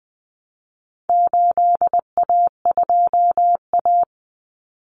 Morse code, heard as a keyed tone, about 20 wpm, 700 Hz. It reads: 8A2A
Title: Common Cut Numbers - 4-Digit 20wpm